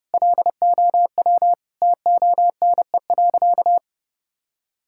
Morse code, heard as a keyed tone, about 32 wpm, 700 Hz. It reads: LOW TONE.
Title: Sentences from Top 1600 Words 30wpm